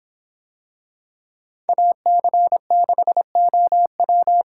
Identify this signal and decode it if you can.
Morse code, heard as a keyed tone, about 26 wpm, 700 Hz. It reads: AC6OW